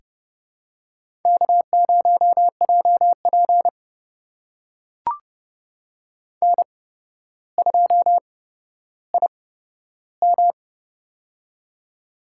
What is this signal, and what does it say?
Morse code, heard as a keyed tone, about 30 wpm, spaced as for 13 wpm, 700 Hz. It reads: K0JP N2IM
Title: USA Call Signs - 8x Character Spacing - 30wpm